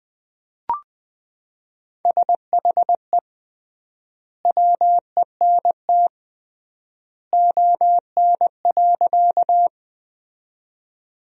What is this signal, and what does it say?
Morse code, heard as a keyed tone, about 20 wpm, 700 Hz. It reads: SHE WENT ON.
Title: 3x Word-Spacing - Sentences from Top 200 Words - 20wpm